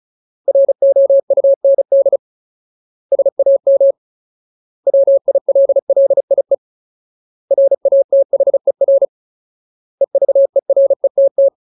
Morse code, heard as a keyed tone, about 34 wpm, 550 Hz. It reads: ROUND SAM WILLIE RATHER EVERETT
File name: Ultimate Word List / 35WPM Ultimate